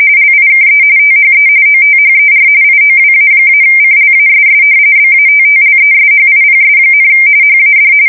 RTTY, about 45.45 baud. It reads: LIKE THE OTHER IOWA-CLASS BATTLESHIPS, HAD
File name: fsk170 / battleshipnj